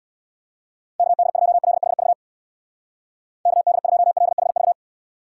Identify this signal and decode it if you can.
Morse code, heard as a keyed tone, about 64 wpm, 700 Hz. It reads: KD2LHV KD2LHV